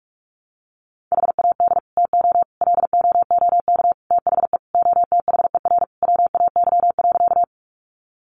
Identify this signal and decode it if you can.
Morse code, heard as a keyed tone, about 44 wpm, 700 Hz. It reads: HAD TO LOOK THE OTHER WAY.
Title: Sentences from Top 100 Words 45wpm